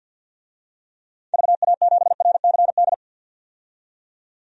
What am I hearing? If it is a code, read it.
Morse code, heard as a keyed tone, about 52 wpm, 700 Hz. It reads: VA7RCD
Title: DXCC-100 Call Signs - 6 Characters - 50wpm